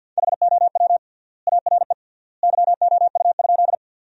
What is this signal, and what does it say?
Morse code, heard as a keyed tone, about 52 wpm, 700 Hz. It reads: HOW ARE YOU?
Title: 3x Word-Spacing - Sentences from Top 100 Words - 50wpm